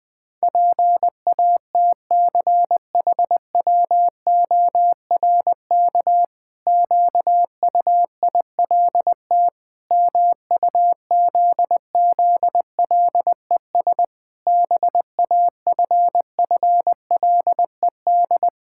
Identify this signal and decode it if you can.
Morse code, heard as a keyed tone, about 20 wpm, 700 Hz. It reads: PATCHWORK QUILT MUZZLES BAFFLED
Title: Pangram Challenge v2 - 20wpm